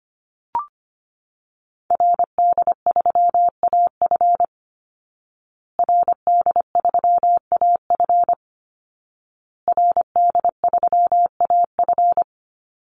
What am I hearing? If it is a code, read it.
Morse code, heard as a keyed tone, about 25 wpm, 700 Hz. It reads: RD3AF RD3AF RD3AF
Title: Call Signs Repeated 2x - 25wpm